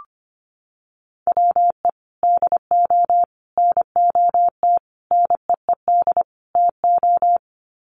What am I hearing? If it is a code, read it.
Morse code, heard as a keyed tone, about 25 wpm, 700 Hz. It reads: WE DO NOT NEED TO
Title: Sets of 5 Words Encore 25wpm